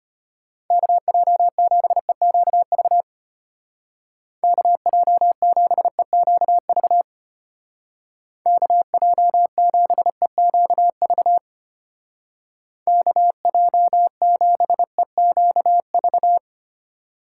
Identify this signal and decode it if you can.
Morse code, heard as a keyed tone, about 31 wpm, 700 Hz. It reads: KJ7EQV KJ7EQV KJ7EQV KJ7EQV
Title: Speed Racing - USA 2x3 Call Signs 25wpm